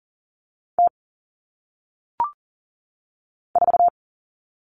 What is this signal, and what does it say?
Morse code, heard as a keyed tone, about 40 wpm, 700 Hz. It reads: T 4
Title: All Cut Numbers - 1-Digit 40wpm